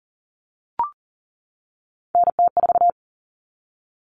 Morse code, heard as a keyed tone, about 40 wpm, 700 Hz. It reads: NT4
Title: All Cut Numbers - 3-Digit 40wpm